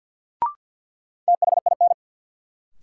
Morse code, heard as a keyed tone, about 49 wpm, 700 Hz. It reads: THIN
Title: Sentences from Top 1200 Words - New Word Drill - 50wpm